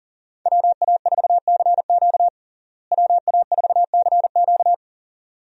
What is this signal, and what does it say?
Morse code, heard as a keyed tone, about 40 wpm, 700 Hz. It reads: WA4CQ WA4CQ
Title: USA Call Signs - 5 Characters - Repeated 2x - 40wpm